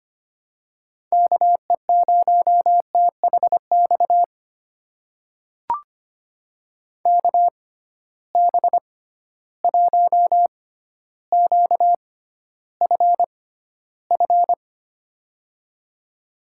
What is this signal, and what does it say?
Morse code, heard as a keyed tone, about 25 wpm, spaced as for 13 wpm, 700 Hz. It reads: KE0THX KB1QFF
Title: USA Call Signs - 6 Characters - 6x Character Spacing - 25wpm